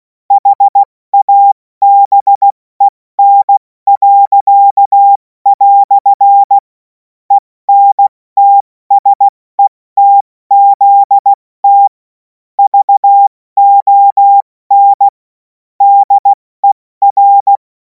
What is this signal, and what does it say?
Morse code, heard as a keyed tone, about 16 wpm, 800 Hz. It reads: HABEN." ENTSETZT VON DER
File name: Morsekurs / Witze / Witz 03. Februar 2026 70 F80